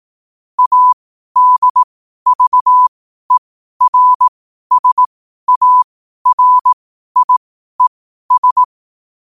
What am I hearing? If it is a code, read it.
Morse code, heard as a keyed tone, about 18 wpm, spaced as for 12 wpm, 1000 Hz. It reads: ADVERSARIES
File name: qotdfiles / 12wpm